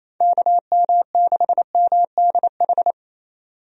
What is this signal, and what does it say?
Morse code, heard as a keyed tone, about 27 wpm, 700 Hz. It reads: KM6MDH
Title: USA Call Signs - 6 Characters - 28wpm